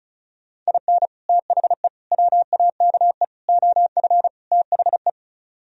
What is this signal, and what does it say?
Morse code, heard as a keyed tone, about 32 wpm, 700 Hz. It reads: IN THE WAKE OF THE
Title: Speed-Racing - Sets of 5 Words 35wpm